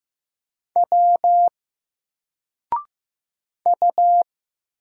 Morse code, heard as a keyed tone, about 15 wpm, 700 Hz. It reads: W U
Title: Single Character (Letter:Number:Punctuation:Prosign) 15wpm